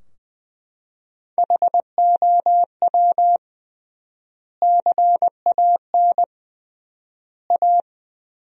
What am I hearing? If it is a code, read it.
Morse code, heard as a keyed tone, about 20 wpm, 700 Hz. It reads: HOW CAN A
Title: 3x Word-Spacing - Sentences from Top 500 Words - 20wpm